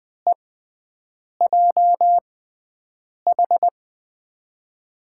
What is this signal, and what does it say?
Morse code, heard as a keyed tone, about 20 wpm, spaced as for 7 wpm, 700 Hz. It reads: EJH
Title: USA Call Signs - 6x Character Spacing - 20wpm